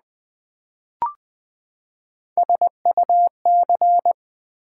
Morse code, heard as a keyed tone, about 21 wpm, 700 Hz. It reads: SUC
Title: Build A Word - 3 Letters At A Time - 9-Letter Words - 20wpm